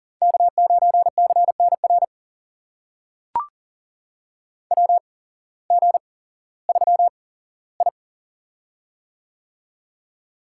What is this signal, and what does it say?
Morse code, heard as a keyed tone, about 40 wpm, spaced as for 19 wpm, 700 Hz. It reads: K9CNR WG3I
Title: USA Call Signs - 8x Character Spacing - 40wpm